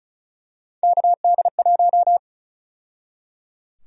Morse code, heard as a keyed tone, about 36 wpm, 700 Hz. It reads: KD1
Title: USA Call Sign Prefixes - 35wpm